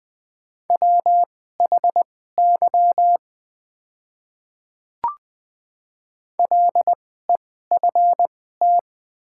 Morse code, heard as a keyed tone, about 20 wpm, spaced as for 14 wpm, 700 Hz. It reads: WHY LEFT
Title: 2x Character Spacing - Top 300 Words - 20wpm